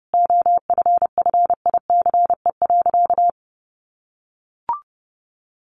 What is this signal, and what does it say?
Morse code, heard as a keyed tone, about 30 wpm, 700 Hz. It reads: OFFICE.
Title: Sentences from Top 1000 Words 30wpm